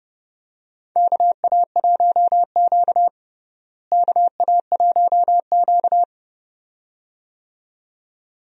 Morse code, heard as a keyed tone, about 30 wpm, 700 Hz. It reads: KA1Q KA1Q
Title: USA Call Signs - 4 Characters - Repeated 2x - 30wpm